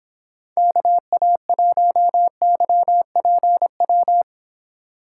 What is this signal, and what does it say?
Morse code, heard as a keyed tone, about 26 wpm, 700 Hz. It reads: KA1YPW